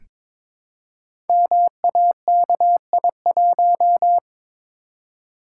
Morse code, heard as a keyed tone, about 23 wpm, 700 Hz. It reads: MAKI1
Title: Code Groups - 5 Letter:Numbers - 22wpm